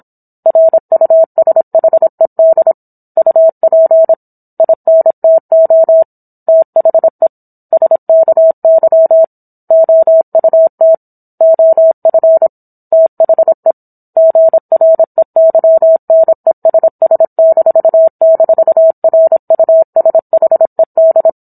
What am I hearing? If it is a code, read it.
Morse code, heard as a keyed tone, about 27 wpm, 650 Hz. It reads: RUSHED UP INTO THE SKY OUT OF THE GREYNESS--RUSHED